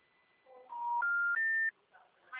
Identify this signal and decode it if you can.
Special information tone.
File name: out- -2050- - - .18094